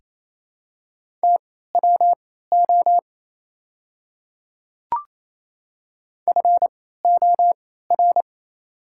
Morse code, heard as a keyed tone, about 28 wpm, spaced as for 16 wpm, 690 Hz. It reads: TWO FOR
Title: 3x Character Spacing - Top 100 Words - 28wpm